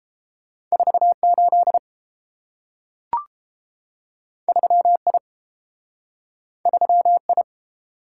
Morse code, heard as a keyed tone, about 30 wpm, 700 Hz. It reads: 48 3I 3I